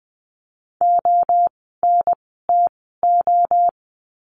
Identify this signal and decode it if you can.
Morse code, heard as a keyed tone, about 20 wpm, spaced as for 14 wpm, 700 Hz. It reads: ONTO
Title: 054 Lesson 10f - Letter r - Words 20wpm